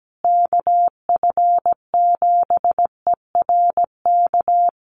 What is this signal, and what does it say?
Morse code, heard as a keyed tone, about 17 wpm, 700 Hz. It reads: KF7ERK
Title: USA Call Signs Repeated 3x 17wpm